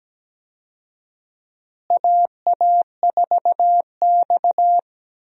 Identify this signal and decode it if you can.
Morse code, heard as a keyed tone, about 17 wpm, 700 Hz. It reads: AA4X
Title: USA Call Signs - 4 Characters - Repeated 3x - 17wpm